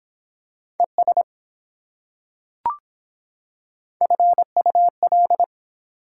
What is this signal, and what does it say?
Morse code, heard as a keyed tone, about 24 wpm, 700 Hz. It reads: ES FUL